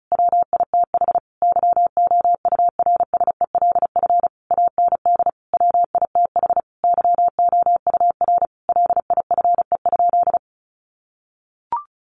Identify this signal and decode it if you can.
Morse code, heard as a keyed tone, about 34 wpm, 700 Hz. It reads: WITH YOURSELF AND WITH YOUR LIFE?
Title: Sentences from Top 900 Words 35wpm